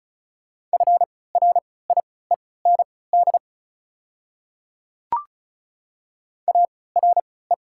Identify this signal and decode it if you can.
Morse code, heard as a keyed tone, about 36 wpm, spaced as for 20 wpm, 700 Hz. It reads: FRIEND ARE
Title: 3x Character Spacing - Top 300 Words - 35wpm